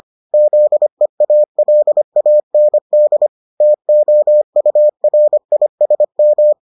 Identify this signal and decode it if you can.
Morse code, heard as a keyed tone, about 25 wpm, 600 Hz. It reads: ZEALAND TOURISM